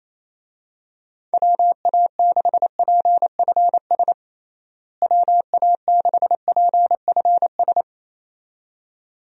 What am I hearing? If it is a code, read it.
Morse code, heard as a keyed tone, about 28 wpm, 700 Hz. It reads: WA6PFS WA6PFS
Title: USA Call Signs - 6 Characters - Repeated 2x - 28wpm